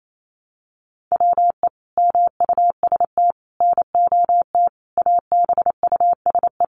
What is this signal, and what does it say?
Morse code, heard as a keyed tone, about 28 wpm, 700 Hz. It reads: WE MUST NOT ABUSE
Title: Sentences from Top 1300 Words 28wpm